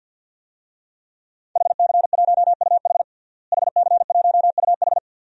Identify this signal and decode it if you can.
Morse code, heard as a keyed tone, about 47 wpm, 700 Hz. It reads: HC1UH HC1UH
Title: DXCC-100 Call Signs - 5 Characters - Repeated 3x - 50wpm